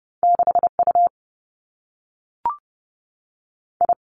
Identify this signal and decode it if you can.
Morse code, heard as a keyed tone, about 30 wpm, 700 Hz. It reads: 6 U I
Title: Two Challenging Letter-Number Combinations 30wpm